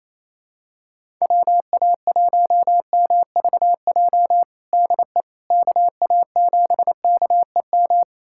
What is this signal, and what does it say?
Morse code, heard as a keyed tone, about 28 wpm, 700 Hz. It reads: WA1MVJ DE KA7KEM
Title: USA Callsign DE Callsign - 6 Characters - 28wpm